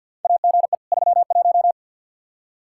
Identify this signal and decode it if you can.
Morse code, heard as a keyed tone, about 46 wpm, 700 Hz. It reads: AGE 31
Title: Practice QSO - Exchange 3 - More Info - 50wpm